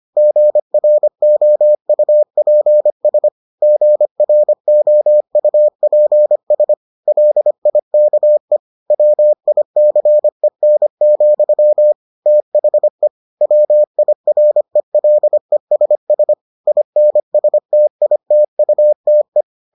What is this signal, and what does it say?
Morse code, heard as a keyed tone, about 25 wpm, 600 Hz. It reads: GROUPS GROUPS LIKE WICEN, THE WIRELESS INSTITUTE